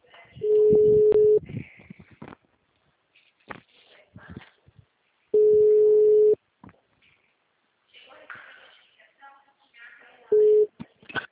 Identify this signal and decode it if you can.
Ringback tone (Europe, 425 Hz), with speech.